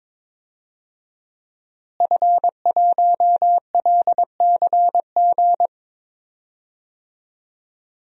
Morse code, heard as a keyed tone, about 22 wpm, 700 Hz. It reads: F1LCG
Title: DXCC-100 Call Signs - 5 Characters - Repeated 3x - 22wpm